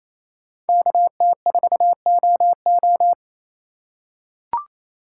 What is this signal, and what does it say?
Morse code, heard as a keyed tone, about 28 wpm, 700 Hz. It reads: KT4OO
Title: USA Call Signs - 3x Character Spacing - 28wpm